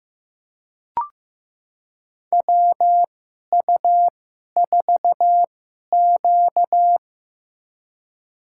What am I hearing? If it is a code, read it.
Morse code, heard as a keyed tone, about 15 wpm, spaced as for 11 wpm, 700 Hz. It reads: WU4Q
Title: USA Call Signs - 4 Characters - 2x Character Spacing - 15wpm